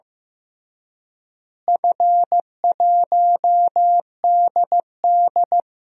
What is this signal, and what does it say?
Morse code, heard as a keyed tone, about 16 wpm, 700 Hz. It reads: F1DD
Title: DXCC-100 Call Signs - 4 Characters - Repeated 2x - 15wpm